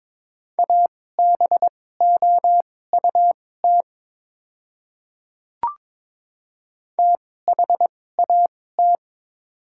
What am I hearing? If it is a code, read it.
Morse code, heard as a keyed tone, about 21 wpm, spaced as for 16 wpm, 700 Hz. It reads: ABOUT THAT
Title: 2x Character Spacing - Top 100 Words - 22wpm